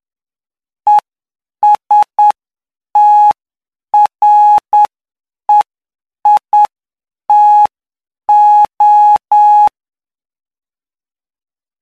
Morse code, heard as a keyed tone, about 9 wpm, spaced as for 7 wpm, 840 Hz. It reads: ESTREITO
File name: cw / py2lcd / 09-PY2LCD-Li��o 18